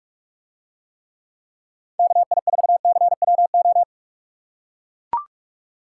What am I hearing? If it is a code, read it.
Morse code, heard as a keyed tone, about 43 wpm, 700 Hz. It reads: KI4CWO